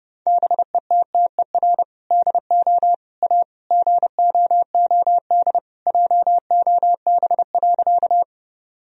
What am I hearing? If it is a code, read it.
Morse code, heard as a keyed tone, about 30 wpm, 700 Hz. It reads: BETTER DO A GOOD JOB.
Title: Sentences from Top 1000 Words - Review Part 3 - 30wpm